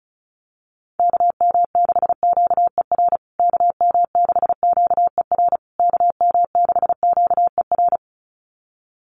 Morse code, heard as a keyed tone, about 34 wpm, 700 Hz. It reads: KM6QER KM6QER KM6QER
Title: USA Call Signs Repeated 3x 35wpm